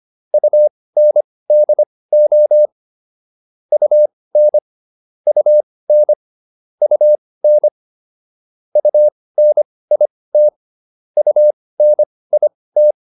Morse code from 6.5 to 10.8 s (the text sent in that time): UN UNIT